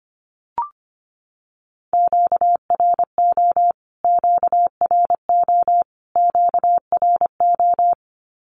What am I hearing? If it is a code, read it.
Morse code, heard as a keyed tone, about 25 wpm, 700 Hz. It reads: QRO QRO QRO
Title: Q-Codes Repeated 3x 25wpm